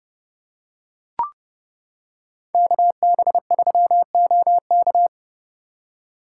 Morse code, heard as a keyed tone, about 29 wpm, 700 Hz. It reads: KB3OK